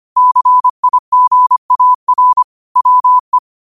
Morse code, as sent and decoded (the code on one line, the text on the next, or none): -.-. .. --. .- .-. / .-- .
CIGAR WE